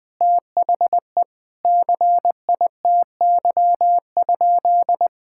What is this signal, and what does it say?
Morse code, heard as a keyed tone, about 20 wpm, 700 Hz. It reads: THE CITY?
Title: Sentences from Top 1600 Words 20wpm